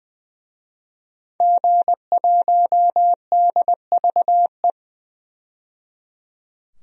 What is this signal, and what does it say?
Morse code, heard as a keyed tone, about 20 wpm, 700 Hz. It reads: G1DVE
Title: DXCC-100 Call Signs - 5 Characters - 20wpm